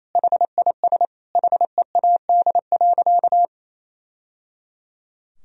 Morse code, heard as a keyed tone, about 28 wpm, 700 Hz. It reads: HIS HEAD.
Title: Sentences from Top 1000 Words - Review Part 3 - 28wpm